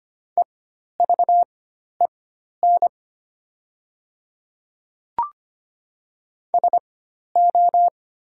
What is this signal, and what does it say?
Morse code, heard as a keyed tone, about 25 wpm, spaced as for 12 wpm, 700 Hz. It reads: EVEN SO
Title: 4x Character Spacing - Top 300 Words - 25wpm